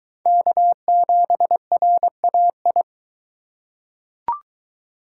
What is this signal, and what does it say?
Morse code, heard as a keyed tone, about 21 wpm, 700 Hz. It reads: K7RAI